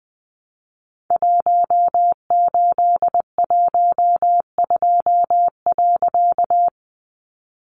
Morse code, heard as a keyed tone, about 20 wpm, 700 Hz. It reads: 1812.
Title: A Sound Of Thunder 20wpm